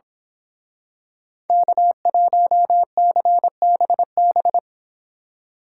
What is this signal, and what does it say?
Morse code, heard as a keyed tone, about 26 wpm, 700 Hz. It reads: K1CBB
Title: Speed Racing - USA 1x3 Call Signs 22wpm